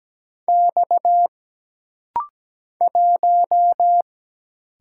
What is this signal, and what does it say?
Morse code, heard as a keyed tone, about 17 wpm, 700 Hz. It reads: X 1
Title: Single Letter-Number Alt - Rapid-Fire 17wpm